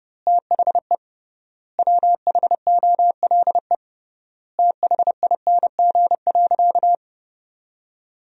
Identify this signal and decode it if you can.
Morse code, heard as a keyed tone, about 30 wpm, 700 Hz. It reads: THE WHOLE THING.
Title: 3x Word-Spacing - Sentences from Top 400 Words - 30wpm